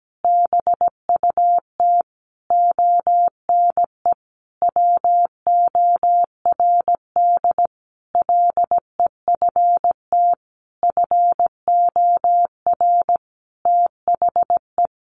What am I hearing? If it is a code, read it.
Morse code, heard as a keyed tone, about 17 wpm, 700 Hz. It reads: BUT ONE WORD LEFT FOR THE
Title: Sentences from Top 1500 Words 17wpm